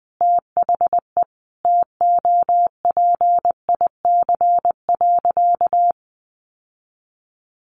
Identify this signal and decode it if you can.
Morse code, heard as a keyed tone, about 20 wpm, 700 Hz. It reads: THE TOPIC.
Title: Sentences from Top 1200 Words 20wpm